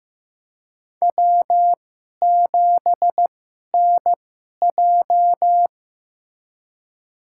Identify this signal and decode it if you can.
Morse code, heard as a keyed tone, about 15 wpm, spaced as for 11 wpm, 700 Hz. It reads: W7NJ
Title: USA Call Signs - 2x Character Spacing - 15wpm